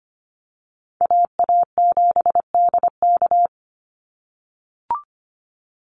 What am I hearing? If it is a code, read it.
Morse code, heard as a keyed tone, about 25 wpm, 700 Hz. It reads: AA7DK
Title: NA SOTA Chaser Callsigns 25wpm